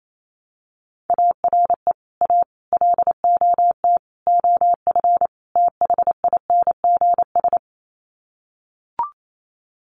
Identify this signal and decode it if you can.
Morse code, heard as a keyed tone, about 28 wpm, 700 Hz. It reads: ARE A LOT OF THINGS
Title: Sets of 5 Words Encore2 28wpm